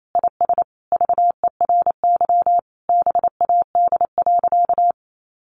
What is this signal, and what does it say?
Morse code, heard as a keyed tone, about 28 wpm, 700 Hz. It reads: IS VERY BAD.
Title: Sentences from Top 1200 Words 28wpm